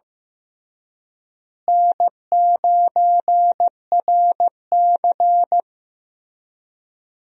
Morse code, heard as a keyed tone, about 16 wpm, 700 Hz. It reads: N9RC
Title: USA Call Signs - 4 Characters - Repeated 3x - 15wpm